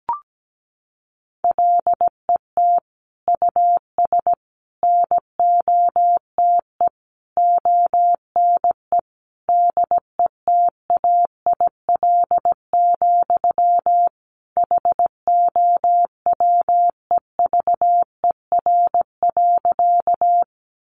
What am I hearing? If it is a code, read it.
Morse code, heard as a keyed tone, about 17 wpm, 700 Hz. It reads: LET US NOTE ONE DETAIL, HOWEVER.
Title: Sentences from Top 1500 Words 17wpm